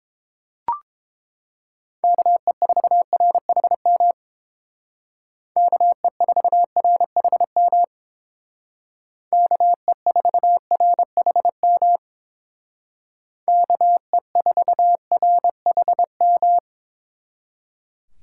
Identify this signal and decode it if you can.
Morse code, heard as a keyed tone, about 28 wpm, 700 Hz. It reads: KE4RHM KE4RHM KE4RHM KE4RHM
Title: Speed Racing - USA Call Signs 22wpm